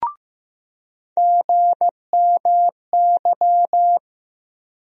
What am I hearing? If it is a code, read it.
Morse code, heard as a keyed tone, about 15 wpm, 700 Hz. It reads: GMY
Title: USA Call Sign Postfixes - 3 Letters 15wpm